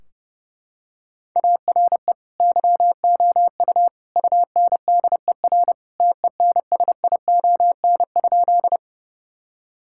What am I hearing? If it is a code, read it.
Morse code, heard as a keyed tone, about 30 wpm, 700 Hz. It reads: ARE YOU UNDER TENSION?
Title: Sentences from Top 1600 Words 30wpm